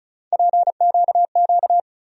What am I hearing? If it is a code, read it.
Morse code, heard as a keyed tone, about 36 wpm, 700 Hz. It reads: PQQ
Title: USA Call Sign Postfixes - 3 Letters 35wpm